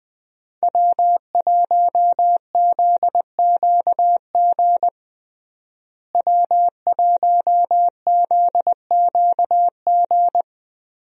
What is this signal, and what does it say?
Morse code, heard as a keyed tone, about 20 wpm, 700 Hz. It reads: W1ZQG W1ZQG
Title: USA Call Signs - 5 Characters - Repeated 3x - 20wpm